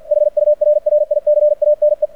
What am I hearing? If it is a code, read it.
Morse code, heard as a keyed tone, about 53 wpm, 600 Hz. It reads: HURRICANE
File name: archived / CRSnet